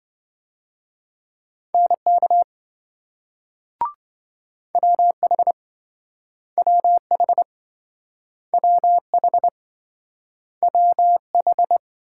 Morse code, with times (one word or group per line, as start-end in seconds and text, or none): 1.8-2.4 NK
4.8-5.5 WH
6.6-7.4 WH
8.5-9.5 WH
10.6-11.8 WH